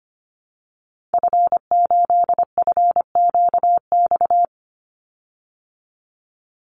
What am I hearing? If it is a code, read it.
Morse code, heard as a keyed tone, about 25 wpm, 700 Hz. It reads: F8FQX
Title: Call Signs Repeated 3x - 25wpm